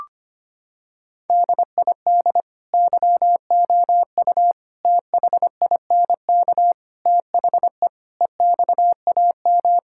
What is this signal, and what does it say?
Morse code, heard as a keyed tone, about 25 wpm, 700 Hz. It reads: DID YOU THINK THE EXAM
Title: Sentences from Top 1000 Words - Review Part 5 - 25wpm